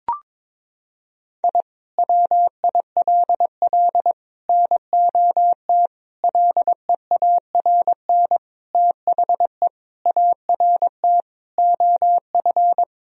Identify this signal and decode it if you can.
Morse code, heard as a keyed tone, about 22 wpm, 700 Hz. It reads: I WILL NOT LEARN THE ART OF
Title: Sentences from Top 1000 Words - Review Part 3 - 22wpm